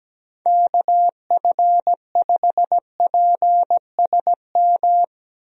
Morse code, heard as a keyed tone, about 17 wpm, 700 Hz. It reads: KF5PSM
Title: USA Call Signs - 6 Characters - Repeated 2x - 17wpm